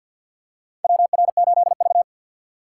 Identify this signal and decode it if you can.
Morse code, heard as a keyed tone, about 51 wpm, 700 Hz. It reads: WR8V
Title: USA Call Signs - 4 Characters - 2x Character Spacing - 50wpm